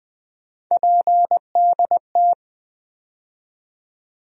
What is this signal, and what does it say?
Morse code, heard as a keyed tone, about 20 wpm, 700 Hz. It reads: PDT
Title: USA Call Sign Postfixes - 3 Letters 20wpm